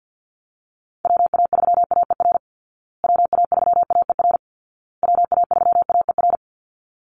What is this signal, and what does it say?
Morse code, heard as a keyed tone, about 50 wpm, 700 Hz. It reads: WA3AER WA3AER WA3AER
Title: Call Signs Repeated 2x - 50wpm